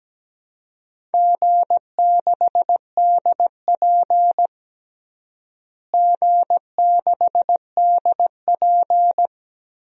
Morse code, heard as a keyed tone, about 17 wpm, 700 Hz. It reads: G6DP G6DP
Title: DXCC-100 Call Signs - 4 Characters - Repeated 2x - 17wpm